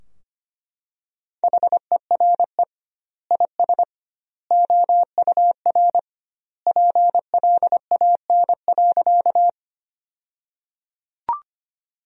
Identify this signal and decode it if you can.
Morse code, heard as a keyed tone, about 25 wpm, 700 Hz. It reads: HERE IS OUR PLAN.
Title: 2x Word-Spacing - Sentences from Top 500 Words - 25wpm